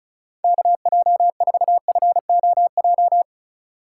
Morse code, heard as a keyed tone, about 33 wpm, 700 Hz. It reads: KJ4FOJ
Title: Speed Racing - USA 2x3 Call Signs 30wpm